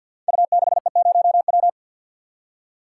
Morse code, heard as a keyed tone, about 51 wpm, 700 Hz. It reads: U6E0W
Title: Code Groups - 5 Letter:Numbers - 50wpm